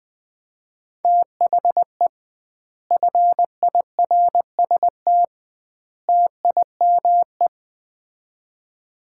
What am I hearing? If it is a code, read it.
Morse code, heard as a keyed tone, about 21 wpm, 700 Hz. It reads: THE FIRST TIME
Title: 2x Word-Spacing - Sets of 3 Words - 20wpm